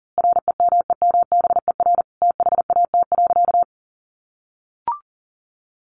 Morse code, heard as a keyed tone, about 40 wpm, 700 Hz. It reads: REMEMBER THAT.
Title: A Sound Of Thunder 40wpm